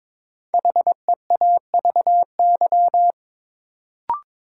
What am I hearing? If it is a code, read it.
Morse code, heard as a keyed tone, about 22 wpm, 700 Hz. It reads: HEAVY
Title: Sentences from Top 1000 Words - Review Part 6 - New Word Drill - 22wpm